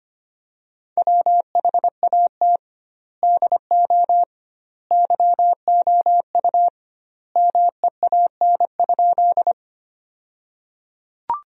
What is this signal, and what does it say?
Morse code, heard as a keyed tone, about 25 wpm, 700 Hz. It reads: WHAT DO YOU MEAN?
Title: 2x Word-Spacing - Sentences from Top 200 Words - 25wpm